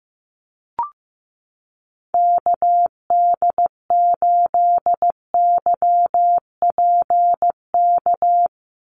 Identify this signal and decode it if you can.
Morse code, heard as a keyed tone, about 15 wpm, 700 Hz. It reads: KD8YPK
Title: USA Call Signs 15wpm